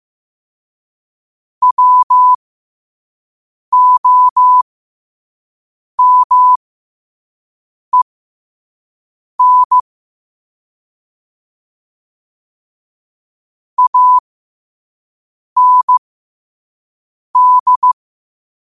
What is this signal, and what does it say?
Morse code, heard as a keyed tone, about 15 wpm, spaced as for 5 wpm, 1000 Hz. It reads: WOMEN AND
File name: qotdfiles / 5wpm